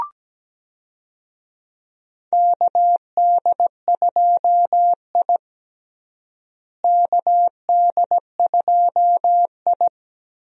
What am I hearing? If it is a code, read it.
Morse code, heard as a keyed tone, about 17 wpm, 700 Hz. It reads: KD2I KD2I
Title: USA Call Signs - 4 Characters - Repeated 2x - 17wpm